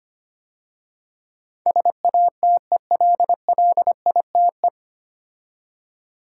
Morse code, heard as a keyed tone, about 25 wpm, 700 Hz. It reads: SATELLITE
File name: practice / 25wpm / 9 Letter Words - First 3 Letters Given - 25wpm